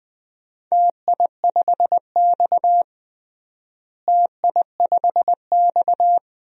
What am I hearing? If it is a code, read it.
Morse code, heard as a keyed tone, about 21 wpm, 700 Hz. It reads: TI5X TI5X
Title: DXCC-100 Call Signs - 4 Characters - Repeated 3x - 20wpm